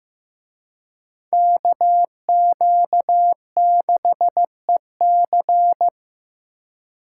Morse code, heard as a keyed tone, about 18 wpm, 700 Hz. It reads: KQ6EC
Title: Speed Racing - USA 2x2 Call Signs 15wpm